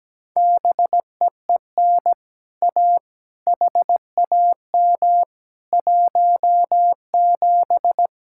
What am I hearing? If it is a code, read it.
Morse code, heard as a keyed tone, about 17 wpm, 700 Hz. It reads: BEEN A HAM 17
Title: Speed-Racing Common QSO Phrases Encore - 17wpm